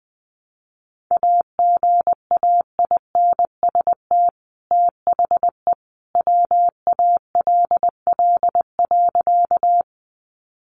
Morse code, heard as a keyed tone, about 20 wpm, 700 Hz. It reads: AGAINST THE WALL.
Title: Sentences from Top 1100 Words 20wpm